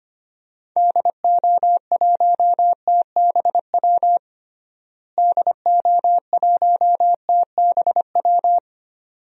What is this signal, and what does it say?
Morse code, heard as a keyed tone, about 25 wpm, 700 Hz. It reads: DO1TBW DO1TBW
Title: DXCC-100 Call Signs - 6 Characters - Repeated 2x - 25wpm